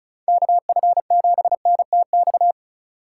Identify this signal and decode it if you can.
Morse code, heard as a keyed tone, about 36 wpm, 700 Hz. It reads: KF7NTX
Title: USA Call Signs - 6 Characters - 3x Character Spacing - 35wpm